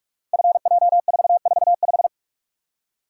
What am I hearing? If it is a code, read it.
Morse code, heard as a keyed tone, about 46 wpm, 700 Hz. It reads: FJ445
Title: Code Groups - 5 Letter:Numbers - 45wpm